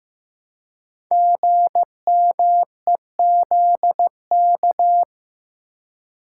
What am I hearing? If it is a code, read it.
Morse code, heard as a keyed tone, about 15 wpm, 700 Hz. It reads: GMEZK
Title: Code Groups - 5 Letter:Numbers - 15wpm